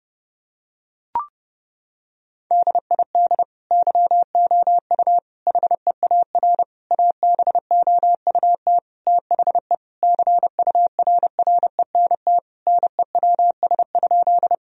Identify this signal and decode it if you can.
Morse code, heard as a keyed tone, about 30 wpm, 700 Hz. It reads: DID YOU HEAR ABOUT THE CURRENT NEWS?
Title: Sentences from Top 1000 Words - Review Part 4 - 30wpm